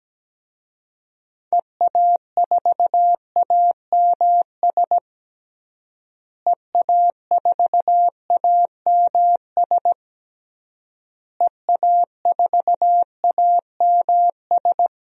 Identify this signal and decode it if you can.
Morse code, heard as a keyed tone, about 17 wpm, 700 Hz. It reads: EA4AMS EA4AMS EA4AMS
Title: DXCC-100 Call Signs - 6 Characters - Repeated 3x - 17wpm